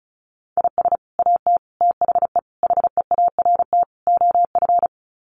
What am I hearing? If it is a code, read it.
Morse code, heard as a keyed tone, about 34 wpm, 700 Hz. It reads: IS AT THE HEART OF
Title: Sets of 5 Words Encore2 35wpm